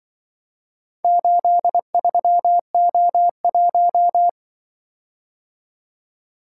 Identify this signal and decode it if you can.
Morse code, heard as a keyed tone, about 24 wpm, 700 Hz. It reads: 83O1
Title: Speed-Racing - Four Letter-Number Combinations - 20wpm